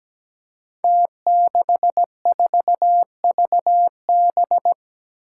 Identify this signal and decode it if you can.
Morse code, heard as a keyed tone, about 17 wpm, 700 Hz. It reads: T64VB
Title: Five Challenging Letter-Number Combinations 17wpm